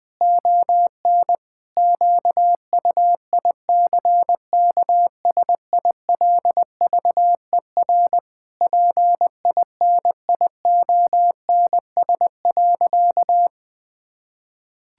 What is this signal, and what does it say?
Morse code, heard as a keyed tone, about 20 wpm, 700 Hz. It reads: ON QUICKSILVER PINIONS.
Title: Pangram Challenge v2 - 20wpm